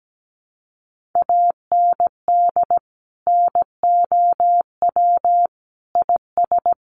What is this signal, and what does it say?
Morse code, heard as a keyed tone, about 17 wpm, 700 Hz. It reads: AND NOW IS
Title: Sentences from Top 800 Words 17wpm